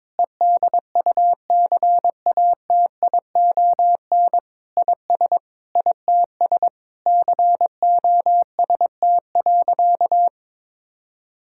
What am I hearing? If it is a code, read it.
Morse code, heard as a keyed tone, about 22 wpm, 700 Hz. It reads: EDUCATION IS ITS COST.
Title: Sentences from Top 1000 Words - Review Part 6 - 22wpm